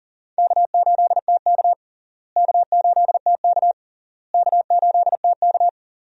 Morse code, heard as a keyed tone, about 40 wpm, 700 Hz. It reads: K8TK K8TK K8TK
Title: USA Call Signs - 4 Characters - Repeated 3x - 40wpm